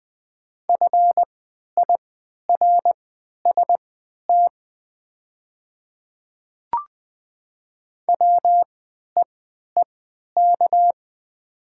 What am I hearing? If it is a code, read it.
Morse code, heard as a keyed tone, about 20 wpm, spaced as for 11 wpm, 700 Hz. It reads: FIRST WEEK
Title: 3x Character Spacing - Top 500 Words - 20wpm